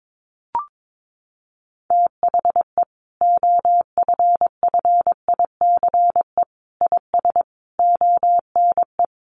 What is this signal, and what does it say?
Morse code, heard as a keyed tone, about 22 wpm, 700 Hz. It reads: THE OFFICE IS ONE
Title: Sentences from Top 1000 Words 22wpm